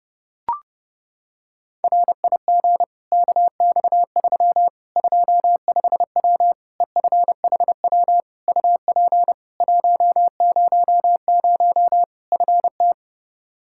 Morse code, heard as a keyed tone, about 32 wpm, 700 Hz. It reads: RIG KX3 25W EFHW UP 100 FT
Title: Practice QSO - Exchange 2 - Info - 30wpm